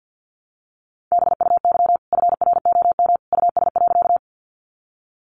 Morse code, heard as a keyed tone, about 50 wpm, 700 Hz. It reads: BUY FROM US.
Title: Sentences from Top 700 Words 50wpm